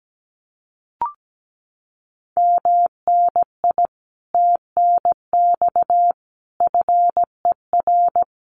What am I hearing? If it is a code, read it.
Morse code, heard as a keyed tone, about 17 wpm, 700 Hz. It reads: MNI TNX FER
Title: Common QSO Phrases Encore 17wpm